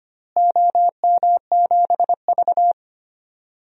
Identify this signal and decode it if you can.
Morse code, heard as a keyed tone, about 25 wpm, 700 Hz. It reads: OM7V
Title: DXCC-100 Call Signs - 4 Characters - Repeated 3x - 25wpm